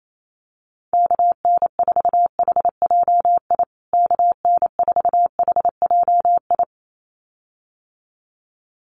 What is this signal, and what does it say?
Morse code, heard as a keyed tone, about 28 wpm, 700 Hz. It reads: KN4HJI KN4HJI
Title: USA Call Signs Repeated 2x 28wpm